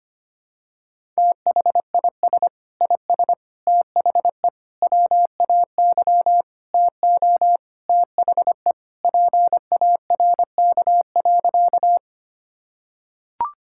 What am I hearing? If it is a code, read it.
Morse code, heard as a keyed tone, about 25 wpm, 700 Hz. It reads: THIS IS THE WAY TO THE PARK.
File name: practice / 25wpm / Sentences with 4-Word Phrases - 25wpm